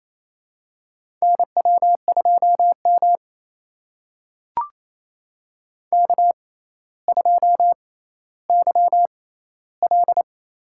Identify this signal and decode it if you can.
Morse code, heard as a keyed tone, about 29 wpm, spaced as for 15 wpm, 700 Hz. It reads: NW2M K2YL
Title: USA Call Signs - 4 Characters - 6x Character Spacing - 28wpm